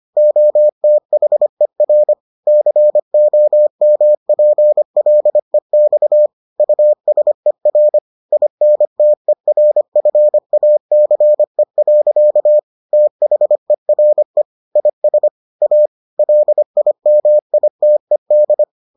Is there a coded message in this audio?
Yes: Morse code from 0.2 to 18.6 s, heard as a keyed tone, about 25 wpm, 600 Hz. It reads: OTHER COMPLEX USER INTERFACE. THERE IS A LIMITED